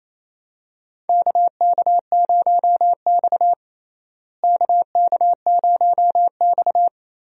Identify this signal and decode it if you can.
Morse code, heard as a keyed tone, about 27 wpm, 700 Hz. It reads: KK0X KK0X
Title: USA Call Signs - 4 Characters - Repeated 3x - 28wpm